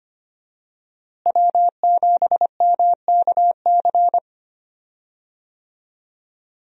Morse code, heard as a keyed tone, about 25 wpm, 700 Hz. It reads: W7MKC
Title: USA Call Signs - 5 Characters - Repeated 3x - 25wpm